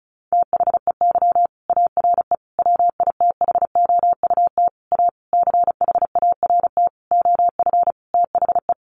Morse code, heard as a keyed tone, about 34 wpm, 700 Hz. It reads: THEY ARE WITHOUT A CHART OF THE
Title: Sentences from Top 1300 Words 35wpm